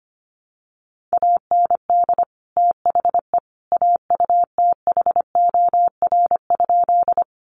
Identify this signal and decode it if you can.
Morse code, heard as a keyed tone, about 25 wpm, 700 Hz. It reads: AND THE AUTHOR?
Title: Sentences from Top 1200 Words 25wpm